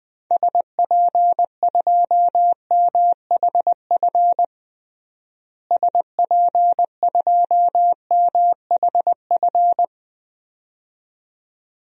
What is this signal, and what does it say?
Morse code, heard as a keyed tone, about 20 wpm, 700 Hz. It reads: SP2MHF SP2MHF
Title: DXCC-100 Call Signs - 6 Characters - Repeated 2x - 20wpm